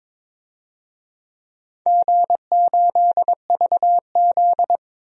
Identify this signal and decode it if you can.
Morse code, heard as a keyed tone, about 22 wpm, 700 Hz. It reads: G8VZ
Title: DXCC-100 Call Signs - 4 Characters - 22wpm